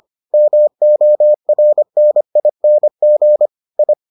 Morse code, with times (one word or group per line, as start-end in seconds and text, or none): 0.3-3.5 MORNING
3.8-3.9 I